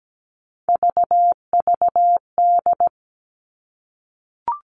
Morse code, heard as a keyed tone, about 17 wpm, 700 Hz. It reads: VVD
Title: Three Challenging Letter-Number Combinations 17wpm